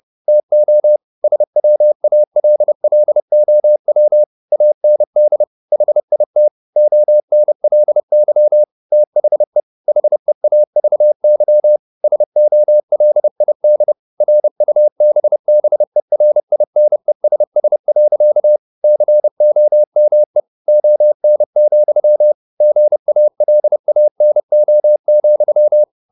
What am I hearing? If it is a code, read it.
Morse code, heard as a keyed tone, about 30 wpm, 600 Hz. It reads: TO SWALLOW AND HIT ONLY THE HEAVY SOLID RUBBERINESS. COME ON, GALANO,